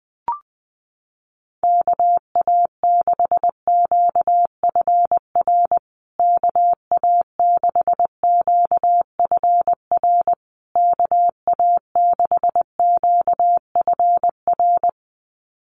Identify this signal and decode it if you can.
Morse code, heard as a keyed tone, about 20 wpm, 700 Hz. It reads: KA6QFR KA6QFR KA6QFR
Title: USA Call Signs Repeated 3x 20wpm